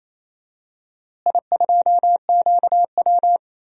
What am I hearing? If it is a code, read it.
Morse code, heard as a keyed tone, about 28 wpm, 700 Hz. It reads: I2QW
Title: DXCC-100 Call Signs - 4 Characters - Repeated 2x - 28wpm